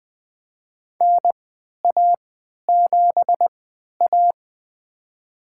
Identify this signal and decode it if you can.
Morse code, heard as a keyed tone, about 20 wpm, spaced as for 11 wpm, 700 Hz. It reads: NA7A
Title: USA Call Signs - 4 Characters - 3x Character Spacing - 20wpm